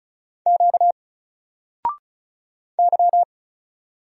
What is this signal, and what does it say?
Morse code, heard as a keyed tone, about 35 wpm, 700 Hz. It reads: Q Y
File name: Single-Letter-Number-35wpm